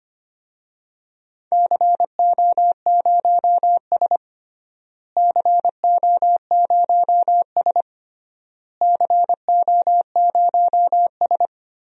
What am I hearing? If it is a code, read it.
Morse code, heard as a keyed tone, about 25 wpm, 700 Hz. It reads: CO0S CO0S CO0S
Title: DXCC-100 Call Signs - 4 Characters - Repeated 3x - 25wpm